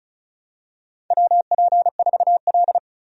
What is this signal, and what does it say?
Morse code, heard as a keyed tone, about 34 wpm, 700 Hz. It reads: WP4L
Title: USA Call Signs - 4 Characters - 35wpm